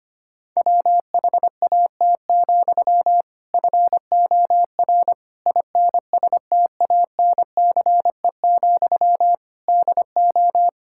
Morse code, heard as a keyed tone, about 25 wpm, 700 Hz. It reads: WHAT, FOR INSTANCE, DO
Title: Sentences from Top 1000 Words - Review Part 5 - 25wpm